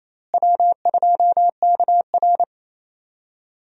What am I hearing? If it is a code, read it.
Morse code, heard as a keyed tone, about 28 wpm, 700 Hz. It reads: W2KR
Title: USA Call Signs - 4 Characters - Repeated 2x - 28wpm